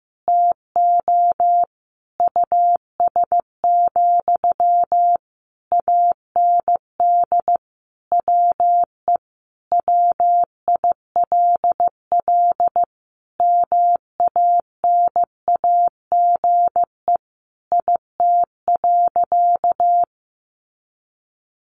Morse code, heard as a keyed tone, about 15 wpm, 700 Hz. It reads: TO US, AND WE WILL MANAGE IT.
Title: Sentences from Top 1400 Words 15wpm